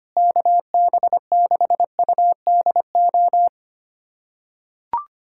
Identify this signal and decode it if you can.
Morse code, heard as a keyed tone, about 25 wpm, 700 Hz. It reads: KB6UDO
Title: USA Callsign DE Callsign 25wpm